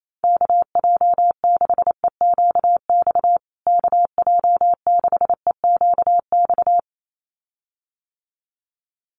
Morse code, heard as a keyed tone, about 28 wpm, 700 Hz. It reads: KJ6EQX KJ6EQX
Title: USA Call Signs Repeated 2x 28wpm